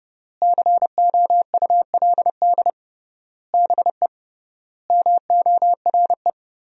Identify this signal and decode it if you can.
Morse code, heard as a keyed tone, about 30 wpm, 700 Hz. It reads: COULD BE MORE
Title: 3x Word-Spacing - Sentences from Top 500 Words - 30wpm